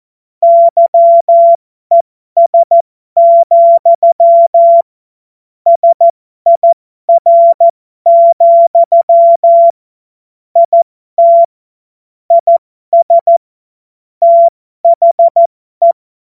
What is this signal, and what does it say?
Morse code, heard as a keyed tone, about 14 wpm, spaced as for 12 wpm, 680 Hz. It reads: YES, SIR, IT IS THE
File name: Morsekurs / Witze